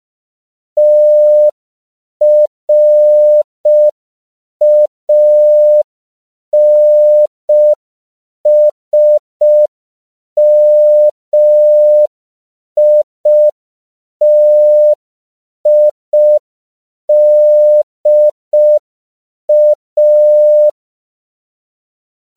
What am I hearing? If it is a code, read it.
Morse code, heard as a keyed tone, about 5 wpm, 600 Hz. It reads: TRANSMITIDA